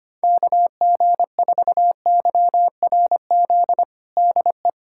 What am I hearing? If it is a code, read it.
Morse code, heard as a keyed tone, about 25 wpm, 700 Hz. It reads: KG4YRZ DE
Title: USA Callsign DE Callsign - 6 Characters - 25wpm